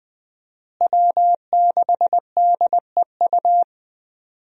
Morse code, heard as a keyed tone, about 21 wpm, 700 Hz. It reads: W6DEU
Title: USA Call Signs - 5 Characters - Repeated 3x - 20wpm